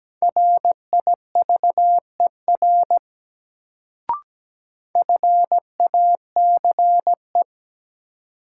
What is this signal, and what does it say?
Morse code, heard as a keyed tone, about 17 wpm, 700 Hz. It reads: RIVER FACE
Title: Sentences from Top 300 Words - New Word Drill - 17wpm